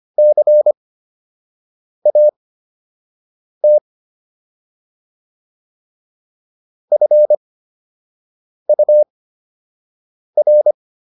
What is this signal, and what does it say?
Morse code, heard as a keyed tone, about 25 wpm, spaced as for 6 wpm, 600 Hz. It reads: CAT FUR